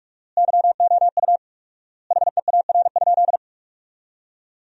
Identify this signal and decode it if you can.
Morse code, heard as a keyed tone, about 46 wpm, 700 Hz. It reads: YOU HEAR?
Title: 4x Word-Spacing - Sentences from Top 300 Words - 45wpm